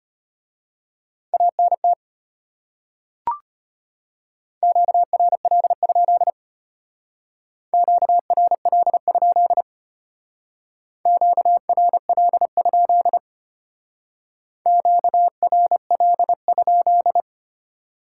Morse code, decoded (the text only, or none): ANT QRL? QRL? QRL? QRL?